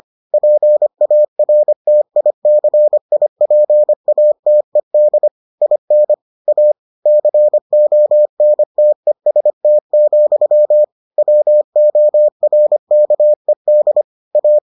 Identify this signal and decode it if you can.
Morse code, heard as a keyed tone, about 25 wpm, 600 Hz. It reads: PARTICIPATED IN A CONTEST, WORKED A